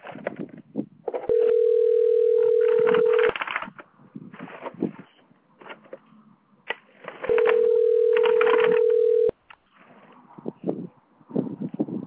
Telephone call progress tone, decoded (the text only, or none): ringback tone